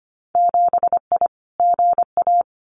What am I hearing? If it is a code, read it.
Morse code, heard as a keyed tone, about 25 wpm, 700 Hz. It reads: 7I GA
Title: Winter Field Day Exchanges Encore 25wpm